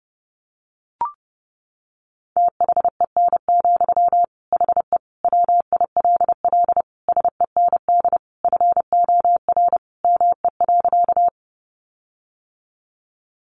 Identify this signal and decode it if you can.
Morse code, heard as a keyed tone, about 28 wpm, 700 Hz. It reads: THEN, HE WILL SEND FOR ME.
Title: Sentences from Top 1500 Words 30wpm